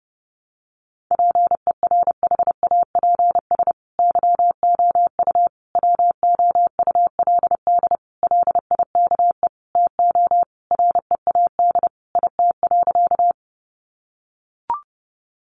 Morse code, heard as a keyed tone, about 30 wpm, 700 Hz. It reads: PERHAPS YOU WOULD LIKE TO READ IT.
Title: Sentences from Top 500 Words 30wpm